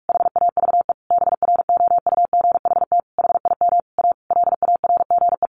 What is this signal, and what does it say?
Morse code, heard as a keyed tone, about 44 wpm, 700 Hz. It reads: HAVE BROUGHT HIM A LARGE
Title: Sentences from Top 1100 Words 45wpm